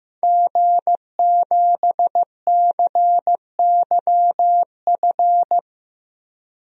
Morse code, heard as a keyed tone, about 15 wpm, 700 Hz. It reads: G7CYF
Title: DXCC-100 Call Signs - 5 Characters - Repeated 2x - 15wpm